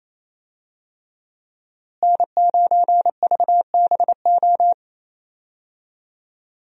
Morse code, heard as a keyed tone, about 28 wpm, 700 Hz. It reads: N9VBO
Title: USA Call Signs - 5 Characters - Repeated 3x - 28wpm